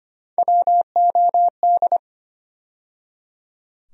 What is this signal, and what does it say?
Morse code, heard as a keyed tone, about 25 wpm, 700 Hz. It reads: WOD
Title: USA Call Sign Postfixes - 3 Letters 25wpm